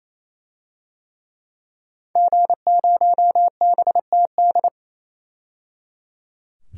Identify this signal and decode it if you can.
Morse code, heard as a keyed tone, about 28 wpm, 700 Hz. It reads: G0BTD
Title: DXCC-100 Call Signs - 5 Characters - 28wpm